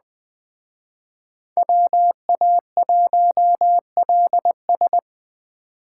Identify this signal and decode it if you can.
Morse code, heard as a keyed tone, about 20 wpm, 700 Hz. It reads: WA1LS
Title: Speed Racing - USA 2x2 Call Signs 17wpm